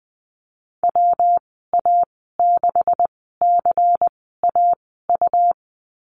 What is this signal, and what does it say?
Morse code, heard as a keyed tone, about 20 wpm, spaced as for 15 wpm, 700 Hz. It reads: WA6CAU
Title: 147 Lesson 24g - Letter y - Call Signs 20wpm